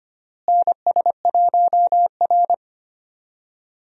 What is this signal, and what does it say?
Morse code, heard as a keyed tone, about 25 wpm, 700 Hz. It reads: NS1R
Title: USA Call Signs - 4 Characters - Repeated 3x - 25wpm